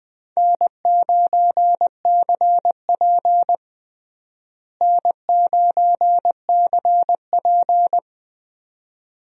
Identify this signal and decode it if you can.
Morse code, heard as a keyed tone, about 20 wpm, 700 Hz. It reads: N9CP N9CP
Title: USA Call Signs - 4 Characters - Repeated 2x - 20wpm